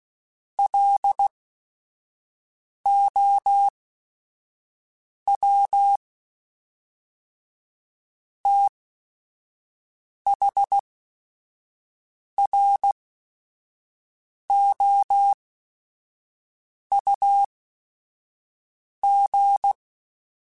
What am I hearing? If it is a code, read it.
Morse code, heard as a keyed tone, about 16 wpm, spaced as for 5 wpm, 800 Hz. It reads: LOW THROUG
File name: Technical Subjects / Technical Subjects-mp3s / Reactance @5